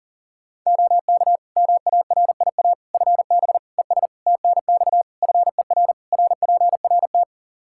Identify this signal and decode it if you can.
Morse code, heard as a keyed tone, about 40 wpm, 700 Hz. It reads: OK MARIA FB ES TNX FER RPRT
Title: Practice QSO - Exchange 2 - Info - 40wpm